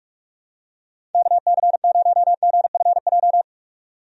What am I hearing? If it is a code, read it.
Morse code, heard as a keyed tone, about 42 wpm, 700 Hz. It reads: KC0GFJ
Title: USA Call Signs - 6 Characters - 45wpm